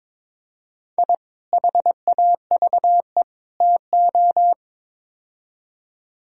Morse code, heard as a keyed tone, about 22 wpm, 700 Hz. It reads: I HAVE TO
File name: practice / 22wpm / Build A Phrase - 5-Word Phrases - 22wpm